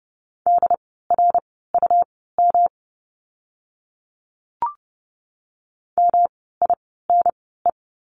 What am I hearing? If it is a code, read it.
Morse code, heard as a keyed tone, about 30 wpm, spaced as for 17 wpm, 700 Hz. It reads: DRUM MINE
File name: course / 30:10wpm / 108 Lesson 18f - Letter m - Words 30wpm